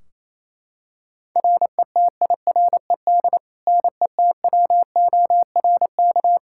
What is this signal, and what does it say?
Morse code, heard as a keyed tone, about 27 wpm, 700 Hz. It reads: RETIRED NETWORK
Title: Practice QSO - Exchange 3 - More Info - 28wpm